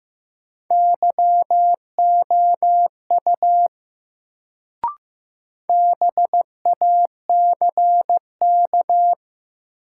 Morse code, heard as a keyed tone, about 15 wpm, 700 Hz. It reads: YOU BACK
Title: Sets of 2 Words - New Word Drill - 15wpm